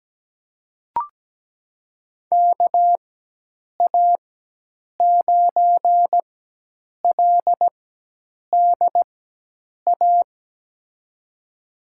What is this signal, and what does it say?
Morse code, heard as a keyed tone, about 17 wpm, spaced as for 8 wpm, 700 Hz. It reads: KA9LDA
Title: USA Call Signs - 4x Character Spacing - 17wpm